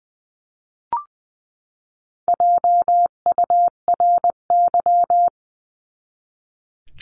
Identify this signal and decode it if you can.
Morse code, heard as a keyed tone, about 20 wpm, 700 Hz. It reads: JURY
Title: 259 Lesson 42d - Words 18-20wpm